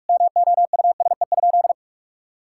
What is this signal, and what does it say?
Morse code, heard as a keyed tone, about 46 wpm, 700 Hz. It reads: MOUSE?
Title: Sentences from Top 1000 Words - Review Part 5 - 45wpm